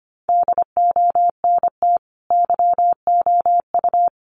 Morse code, heard as a keyed tone, about 25 wpm, 700 Hz. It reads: DONT YOU
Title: Sentences from Top 300 Words 25wpm